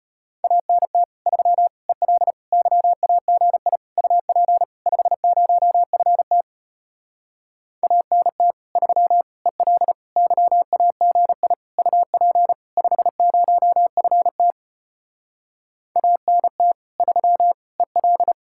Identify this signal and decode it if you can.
Morse code, heard as a keyed tone, about 33 wpm, 700 Hz. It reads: ANT 3 EL YAGI UP 50FT ANT 3 EL YAGI UP 50FT ANT 3 EL